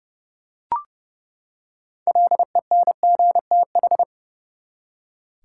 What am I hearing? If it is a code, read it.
Morse code, heard as a keyed tone, about 30 wpm, 700 Hz. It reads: LENGTH
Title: Six Letter Words - 30wpm